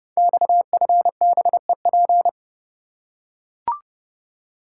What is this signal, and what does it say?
Morse code, heard as a keyed tone, about 31 wpm, 700 Hz. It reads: XFBEP
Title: Code Groups - 5 Letter:Numbers - 30wpm